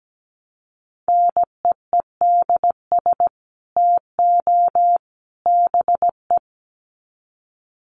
Morse code, heard as a keyed tone, about 17 wpm, 700 Hz. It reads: NEEDS TO BE
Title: Sets of 3 Words 17wpm